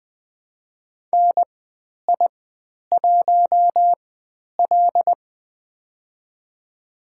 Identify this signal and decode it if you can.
Morse code, heard as a keyed tone, about 20 wpm, spaced as for 10 wpm, 700 Hz. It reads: NI1L
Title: 095 Lesson 16g - Letter c - Call Signs 20wpm